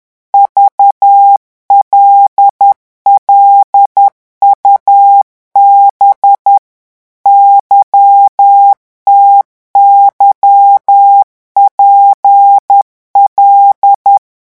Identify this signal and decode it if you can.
Morse code, heard as a keyed tone, about 10 wpm, 800 Hz. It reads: VLLUB YTYPL
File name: morse mp3 / revision11 lettres vitesse 10